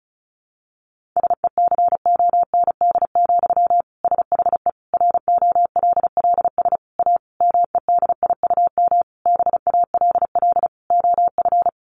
Morse code, heard as a keyed tone, about 34 wpm, 700 Hz. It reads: SECOND, SHE ROLLS A MEDIUM BALL OF
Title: Easy Phrases 35wpm